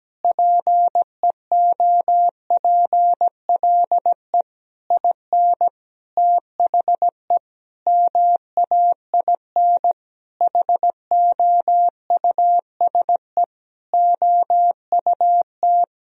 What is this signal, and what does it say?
Morse code, heard as a keyed tone, about 17 wpm, 700 Hz. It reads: PEOPLE IN THE MAIN HOUSE OUT
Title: Sentences from Top 1000 Words - Review Part 2 - 17wpm